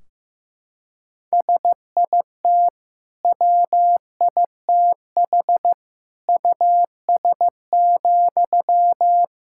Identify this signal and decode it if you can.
Morse code, heard as a keyed tone, about 15 wpm, 700 Hz. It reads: SIT WITH US,
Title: Sentences from Top 1600 Words 15wpm